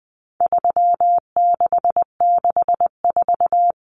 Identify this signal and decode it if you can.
Morse code, heard as a keyed tone, about 20 wpm, 700 Hz. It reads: 3664
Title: Common Cut Numbers - 4-Digit 20wpm